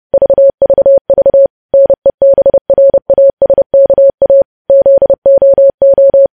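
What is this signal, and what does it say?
Morse code, heard as a keyed tone, about 30 wpm, 550 Hz. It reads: VVV NEBRASKA ZOO
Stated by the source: News headlines sent as Morse practice at 30.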